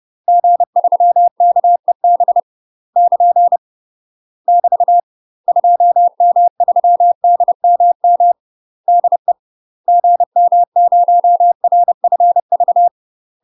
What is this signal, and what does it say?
Morse code, heard as a keyed tone, about 30 wpm, 700 Hz. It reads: G3KEB <KN> <BT> 2M3DMM DE GM0RFV
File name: UK MASTER QSOs TRAINING / 30WPM / UK MASTER QSOs 0011